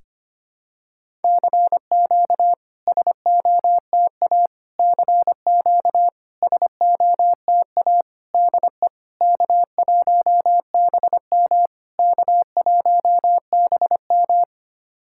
Morse code, heard as a keyed tone, about 25 wpm, 690 Hz. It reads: CQ SOTA CQ SOTA DE K1BM K1BM
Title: SOTA QSO Practice 25wpm